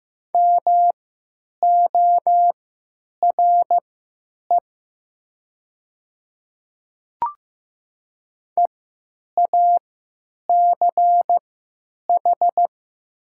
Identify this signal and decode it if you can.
Morse code, heard as a keyed tone, about 15 wpm, spaced as for 9 wpm, 700 Hz. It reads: MORE EACH
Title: 3x Character Spacing - Top 100 Words - 15wpm